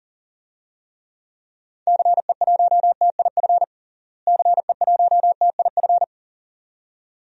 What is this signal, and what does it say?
Morse code, heard as a keyed tone, about 40 wpm, 700 Hz. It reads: CE1TIF CE1TIF
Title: DXCC-100 Call Signs - 6 Characters - Repeated 2x - 40wpm